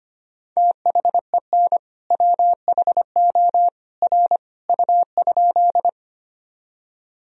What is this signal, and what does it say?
Morse code, heard as a keyed tone, about 25 wpm, 700 Hz. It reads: THEN WHO R U?
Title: Sentences from Top 100 Words and Abbreviations 25wpm